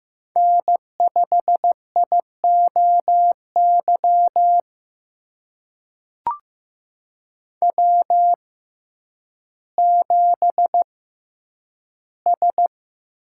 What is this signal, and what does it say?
Morse code, heard as a keyed tone, about 15 wpm, spaced as for 9 wpm, 700 Hz. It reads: N5IOY W7S
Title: USA Call Signs - 5 Characters - 6x Character Spacing - 15wpm